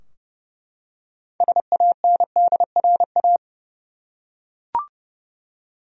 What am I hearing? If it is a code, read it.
Morse code, heard as a keyed tone, about 30 wpm, 700 Hz. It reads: SANDRA
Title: USA Common Names - 30wpm